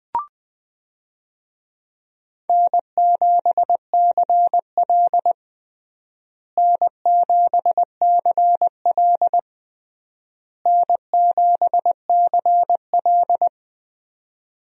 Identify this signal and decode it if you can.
Morse code, heard as a keyed tone, about 20 wpm, 700 Hz. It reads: N7CL N7CL N7CL
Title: USA Call Signs - 4 Characters - Repeated 3x - 20wpm